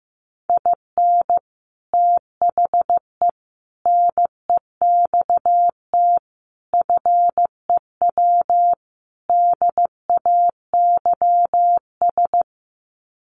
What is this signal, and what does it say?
Morse code, heard as a keyed tone, about 15 wpm, 700 Hz. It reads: IN THE NEXT FEW DAYS
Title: Sets of 5 Words Encore2 15wpm